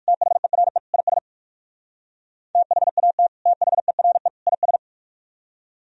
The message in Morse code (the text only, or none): THERE IS THAT THERE IS